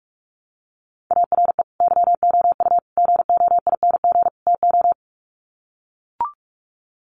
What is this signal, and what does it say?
Morse code, heard as a keyed tone, about 44 wpm, 700 Hz. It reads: ARE YOU GOING TO
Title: Sets of 4 Words 45wpm